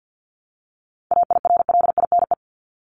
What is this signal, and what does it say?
Morse code, heard as a keyed tone, about 50 wpm, 700 Hz. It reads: AIRLINE
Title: Top 1501-1750 Words 50wpm